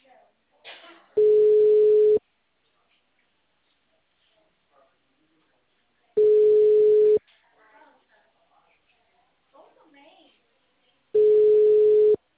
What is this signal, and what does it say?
Ringback tone (Europe, 425 Hz), with speech.